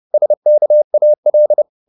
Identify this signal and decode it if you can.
Morse code, heard as a keyed tone, about 30 wpm, 600 Hz. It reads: SKAL
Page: Ferie 30 wpm